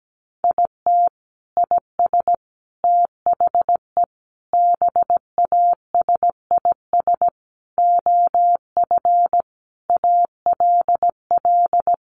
Morse code, heard as a keyed tone, about 17 wpm, 700 Hz. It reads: IT IS THE BASIS OF ALL
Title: Sentences from Top 1200 Words 17wpm